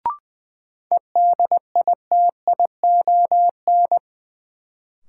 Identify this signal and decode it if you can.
Morse code, heard as a keyed tone, about 20 wpm, 700 Hz. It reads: EDITION
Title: Sets of 5 Words - New Word Drill - 20wpm